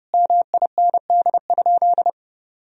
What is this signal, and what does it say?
Morse code, heard as a keyed tone, about 30 wpm, 700 Hz. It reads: MIND?
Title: Sentences from Top 1000 Words - Review Part 3 - 30wpm